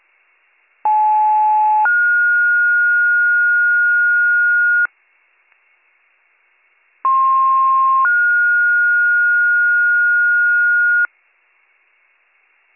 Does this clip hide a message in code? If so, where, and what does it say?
Yes: two-tone page from 0.8 to 11.1 s, heard as a one-second tone, then a three-second tone, about 860 then 1430 Hz, about 1060 then 1430 Hz.